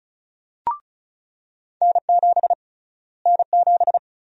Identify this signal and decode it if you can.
Morse code, heard as a keyed tone, about 36 wpm, 700 Hz. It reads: N7 N7
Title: USA Call Sign Prefixes - Repeated 2x - 35wpm